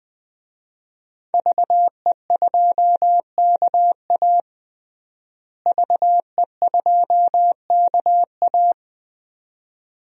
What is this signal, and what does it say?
Morse code, heard as a keyed tone, about 21 wpm, 700 Hz. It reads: VE2KA VE2KA
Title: DXCC-100 Call Signs - 5 Characters - Repeated 2x - 20wpm